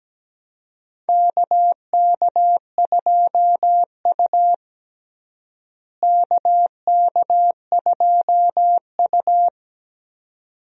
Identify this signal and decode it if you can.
Morse code, heard as a keyed tone, about 17 wpm, 700 Hz. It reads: KK2U KK2U
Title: USA Call Signs - 4 Characters - Repeated 2x - 17wpm